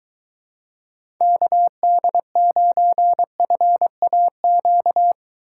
Morse code, heard as a keyed tone, about 23 wpm, 700 Hz. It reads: KD9FAQ